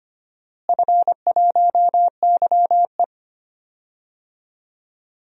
Morse code, heard as a keyed tone, about 25 wpm, 700 Hz. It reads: F1YE
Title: DXCC-100 Call Signs - 4 Characters - Repeated 3x - 25wpm